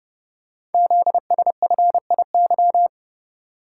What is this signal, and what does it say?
Morse code, heard as a keyed tone, about 29 wpm, 700 Hz. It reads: ZSFIY
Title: Code Groups - 5 Letters - 30wpm